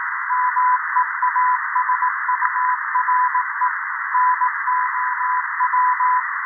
Morse code, heard as a keyed tone, about 18 wpm, 1000 Hz, with noise. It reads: MEASURE NOW
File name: 0340z-fmt k5cm 80m by pa0pje